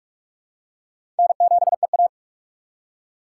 Morse code, heard as a keyed tone, about 46 wpm, 700 Hz. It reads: N7EA
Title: USA Call Signs - 4 Characters - 6x Character Spacing - 45wpm